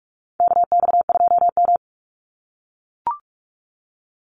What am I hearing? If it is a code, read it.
Morse code, heard as a keyed tone, about 44 wpm, 700 Hz. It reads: KX2M
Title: USA Call Signs 45wpm